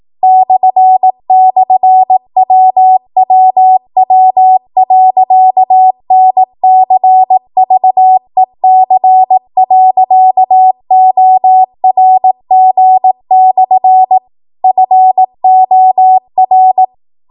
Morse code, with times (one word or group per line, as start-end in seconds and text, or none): 0.2-14.2 //WWW.NCVEC.ORG/
14.6-16.8 FOR